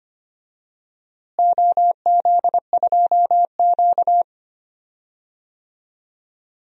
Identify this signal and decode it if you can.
Morse code, heard as a keyed tone, about 25 wpm, 700 Hz. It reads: OZ2Q
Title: DXCC-100 Call Signs - 4 Characters - Repeated 2x - 25wpm